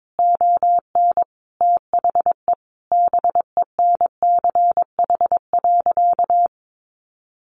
Morse code, heard as a keyed tone, about 22 wpm, 700 Hz. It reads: ON THE BENCH.
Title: Sentences from Top 1400 Words 22wpm